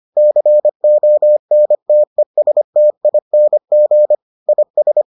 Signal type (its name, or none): Morse code